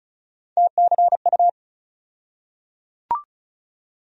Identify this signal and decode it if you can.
Morse code, heard as a keyed tone, about 36 wpm, 700 Hz. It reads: TCU
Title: USA Call Sign Postfixes - 3 Letters 35wpm